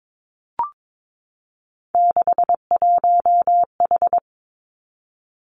Morse code, heard as a keyed tone, about 22 wpm, 700 Hz. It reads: 61H
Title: Three Challenging Letter-Number Combinations 22wpm